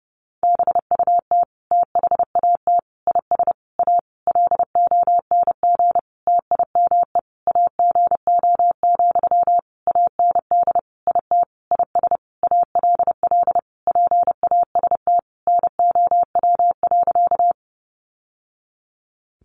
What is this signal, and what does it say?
Morse code, heard as a keyed tone, about 30 wpm, 700 Hz. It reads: BUT THAT IS A LONG TIME AGO, AND IT IS ALL PAST NOW.
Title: Sentences from Top 600 Words 30wpm